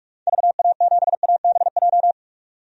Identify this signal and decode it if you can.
Morse code, heard as a keyed tone, about 42 wpm, 700 Hz. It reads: VA7ABJ
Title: DXCC-100 Call Signs - 6 Characters - 45wpm